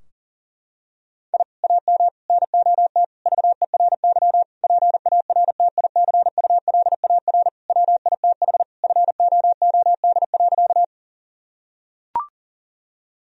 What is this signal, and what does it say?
Morse code, heard as a keyed tone, about 40 wpm, 700 Hz. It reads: I AM NOT VERY PARTICULAR WITH FOOD.
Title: Sentences from Top 1000 Words - Review Part 4 - 40wpm